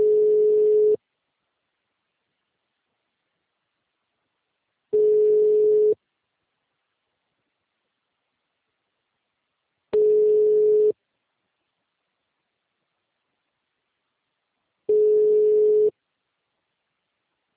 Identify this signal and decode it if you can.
Ringback tone (Europe, 425 Hz).
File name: out- -2006- - - .128